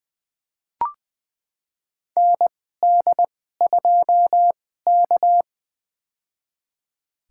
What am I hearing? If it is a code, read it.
Morse code, heard as a keyed tone, about 20 wpm, spaced as for 14 wpm, 700 Hz. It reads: ND2K
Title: USA Call Signs - 2x Character Spacing - 20wpm